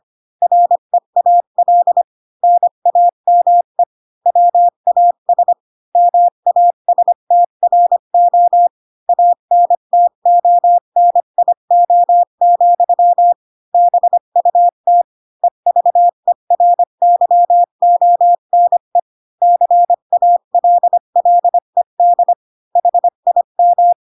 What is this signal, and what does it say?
Morse code, heard as a keyed tone, about 25 wpm, 700 Hz. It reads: REAL NAME WAS MASTRO ANTONIO, BUT EVERYONE CALLED HIM
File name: Pinocchio / Pinocchio ch1 20 25 ik1hge